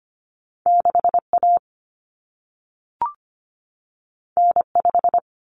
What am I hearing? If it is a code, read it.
Morse code, heard as a keyed tone, about 25 wpm, 700 Hz. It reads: 6A N5
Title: Common Cut Numbers - 2-Digit 25wpm